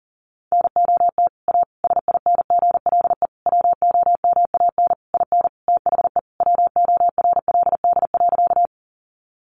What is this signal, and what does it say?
Morse code, heard as a keyed tone, about 40 wpm, 700 Hz. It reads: NOT A SINGLE WOMAN IN THE WORLD.
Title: Sentences from Top 1100 Words 40wpm